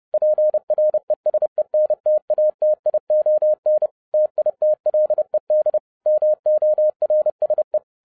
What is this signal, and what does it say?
Morse code, heard as a keyed tone, about 30 wpm, 610 Hz. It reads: PRESENTATION TITLED MORSE